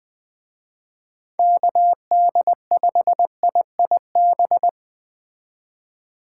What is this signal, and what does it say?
Morse code, heard as a keyed tone, about 24 wpm, 700 Hz. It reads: KD5IIB
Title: Speed Racing - USA 2x3 Call Signs 20wpm